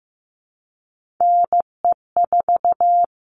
Morse code, heard as a keyed tone, about 15 wpm, 700 Hz. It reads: NE4
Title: All Cut Numbers - 3-Digit 15wpm